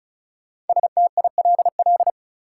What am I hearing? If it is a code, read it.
Morse code, heard as a keyed tone, about 35 wpm, 700 Hz. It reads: STILL
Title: Sentences from Top 300 Words - New Word Drill - 35wpm